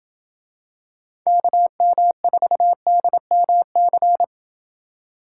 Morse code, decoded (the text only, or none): KM4DMC